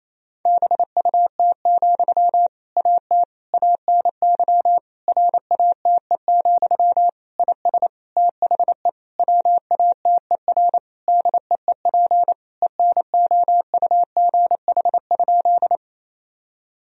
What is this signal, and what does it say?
Morse code, heard as a keyed tone, about 28 wpm, 700 Hz. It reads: BUT, AT ANY RATE, IS THE WATER DEEP ENOUGH?
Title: Sentences from Top 1000 Words - Review Part 4 - 28wpm